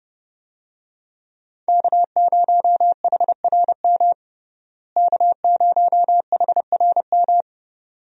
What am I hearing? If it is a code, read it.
Morse code, heard as a keyed tone, about 32 wpm, 700 Hz. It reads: K0HRM K0HRM
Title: USA Call Signs - 5 Characters - Repeated 2x - 30wpm